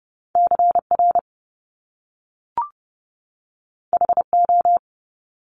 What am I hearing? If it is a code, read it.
Morse code, heard as a keyed tone, about 30 wpm, 700 Hz. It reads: CR HO
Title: Top Two Letter Combinations in Words 30wpm